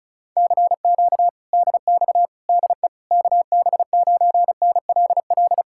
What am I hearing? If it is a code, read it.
Morse code, heard as a keyed tone, about 40 wpm, 700 Hz. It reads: CQ DX DE KB9NLL